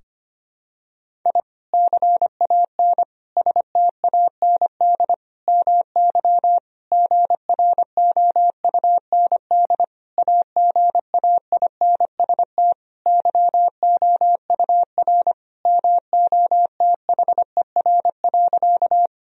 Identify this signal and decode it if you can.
Morse code, heard as a keyed tone, about 25 wpm, 700 Hz. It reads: I CAN STAND MY GROUND AGAINST YOUR MOTHER.
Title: Sentences from Top 1000 Words - Review Part 2 - 25wpm